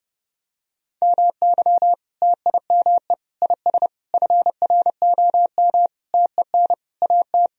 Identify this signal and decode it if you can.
Morse code, heard as a keyed tone, about 32 wpm, 700 Hz. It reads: MY TIME IS FROM TEN AT
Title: Sentences from Top 1000 Words - Review Part 2 - 30wpm